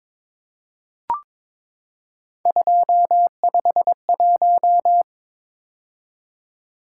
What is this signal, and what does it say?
Morse code, heard as a keyed tone, about 22 wpm, 700 Hz. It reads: 251
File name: practice / 22wpm / Three Letter-Number Combinations - 22wpm